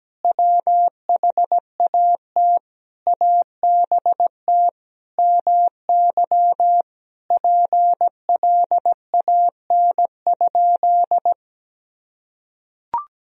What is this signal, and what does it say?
Morse code, heard as a keyed tone, about 17 wpm, 700 Hz. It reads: WHAT ABT MY PLAN?
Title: Sentences from Top 500 Words and Abbreviations 17wpm